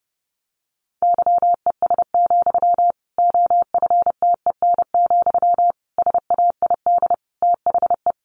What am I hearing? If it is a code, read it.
Morse code, heard as a keyed tone, about 30 wpm, 700 Hz. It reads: YES, OFTEN, SAID THE
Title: Sentences from Top 1200 Words 30wpm